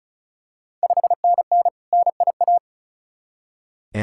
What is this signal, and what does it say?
Morse code, heard as a keyed tone, about 36 wpm, 690 Hz. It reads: 5NN NIA
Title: Ontario QSO Party Exchange 35wpm